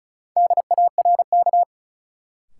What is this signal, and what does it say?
Morse code, heard as a keyed tone, about 34 wpm, 700 Hz. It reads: DARK
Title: Sentences from Top 500 Words - New Word Drill - 35wpm